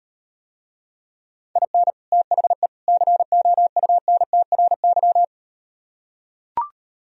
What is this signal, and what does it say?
Morse code, heard as a keyed tone, about 38 wpm, 700 Hz. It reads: IN THE COUNTRY